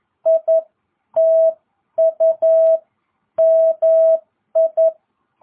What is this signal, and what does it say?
Morse code, heard as a keyed tone, about 11 wpm, 650 Hz. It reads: I TU MI